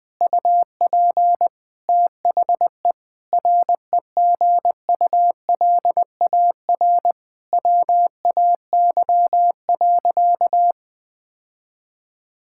Morse code, heard as a keyed tone, about 20 wpm, 700 Hz. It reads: UP THE REGULAR WAY.
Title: Sentences from Top 1000 Words - Review Part 6 - 20wpm